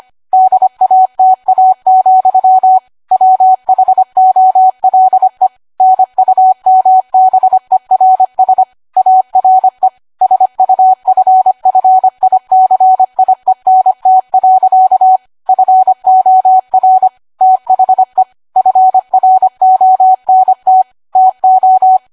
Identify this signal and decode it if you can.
Morse code, heard as a keyed tone, about 25 wpm, 750 Hz. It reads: DATA, WHOLE NUMBERS ARE SUFFICIENT. FOR THE FRONT TO